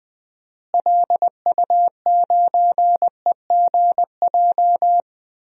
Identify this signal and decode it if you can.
Morse code, heard as a keyed tone, about 20 wpm, 700 Hz. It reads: LU9EGJ
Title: DXCC-100 Call Signs - 6 Characters - Repeated 2x - 20wpm